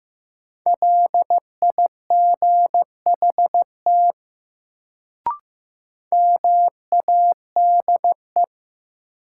Morse code, heard as a keyed tone, about 15 wpm, 700 Hz. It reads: LIGHT MADE
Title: Sentences from Top 200 Words - New Word Drill - 15wpm